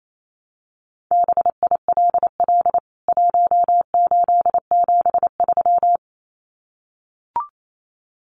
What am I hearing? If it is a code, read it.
Morse code, heard as a keyed tone, about 28 wpm, 700 Hz. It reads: BILL 1873
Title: CWT Exchange 28wpm